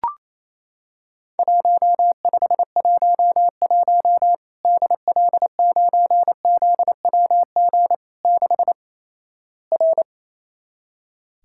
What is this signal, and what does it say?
Morse code, heard as a keyed tone, about 28 wpm, 700 Hz. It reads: 1511 DL9ZWG 6 R
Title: Worked All Europe DX Contest - QTC Exchange - 28wpm